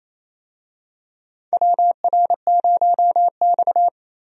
Morse code, heard as a keyed tone, about 27 wpm, 700 Hz. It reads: WR0X
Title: USA Call Signs - 4 Characters - 28wpm